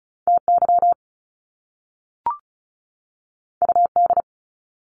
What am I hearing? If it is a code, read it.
Morse code, heard as a keyed tone, about 36 wpm, 700 Hz. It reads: TY UD
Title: Top Two Letter Combinations in Words 35wpm